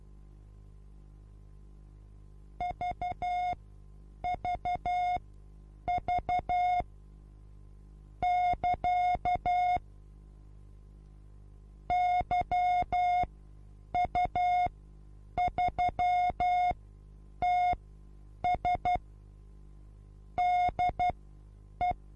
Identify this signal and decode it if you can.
Morse code, heard as a keyed tone, about 12 wpm, spaced as for 8 wpm, 720 Hz. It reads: VVV <KA> YU3TS DE
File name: Onlinelehrgang / morsen / DJ4UF Morsen23